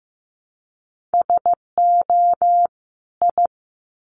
Morse code, heard as a keyed tone, about 15 wpm, 700 Hz. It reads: SO I
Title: Sets of 2 Words 15wpm